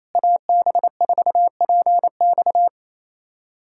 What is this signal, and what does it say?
Morse code, heard as a keyed tone, about 28 wpm, 700 Hz. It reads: AB4PX
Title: USA Callsign DE Callsign - 5 Characters - 28wpm